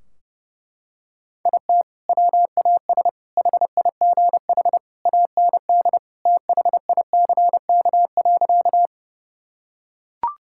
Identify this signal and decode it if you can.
Morse code, heard as a keyed tone, about 32 wpm, 700 Hz. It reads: IT WAS HIGH AND THICK.
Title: Sentences from Top 1600 Words 30wpm